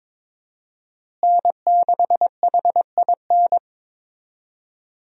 Morse code, heard as a keyed tone, about 22 wpm, 700 Hz. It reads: N6HIN
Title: USA Call Signs - 5 Characters - Repeated 2x - 22wpm